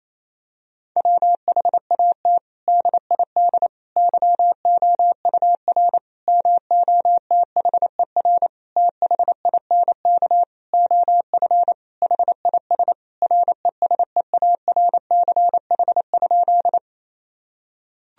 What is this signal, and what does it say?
Morse code, heard as a keyed tone, about 28 wpm, 700 Hz. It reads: WHAT DID YOUR MOTHER THINK OF HIS RESEARCH?
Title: Sentences from Top 1000 Words - Review Part 4 - 28wpm